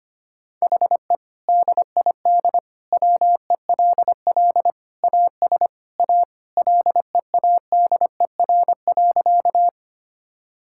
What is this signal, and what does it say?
Morse code, heard as a keyed tone, about 25 wpm, 700 Hz. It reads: HE DID WELL AS A LEADER.
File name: practice / 25wpm / Sentences with 2-Word Phrases - 25wpm